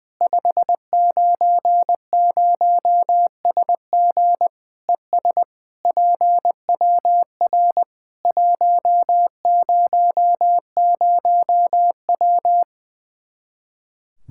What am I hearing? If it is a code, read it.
Morse code, heard as a keyed tone, about 20 wpm, 700 Hz. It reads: 590SG ES PWR 100W
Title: K1USN SST QSO Practice 20wpm